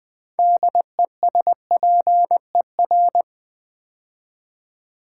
Morse code, heard as a keyed tone, about 20 wpm, 700 Hz. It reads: DESPER
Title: Build A Word - 3 Letters At A Time - 9-Letter Words - 20wpm